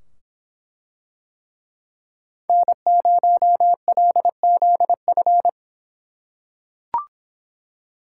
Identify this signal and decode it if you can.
Morse code, heard as a keyed tone, about 24 wpm, 700 Hz. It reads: N0LZF